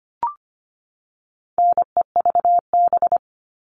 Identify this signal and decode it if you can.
Morse code, heard as a keyed tone, about 25 wpm, 700 Hz. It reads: NEVB
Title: All Cut Numbers - 4-Digit 25wpm